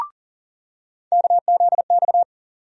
Morse code, heard as a keyed tone, about 37 wpm, 700 Hz. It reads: KZX
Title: USA Call Sign Postfixes - 3 Letters 40wpm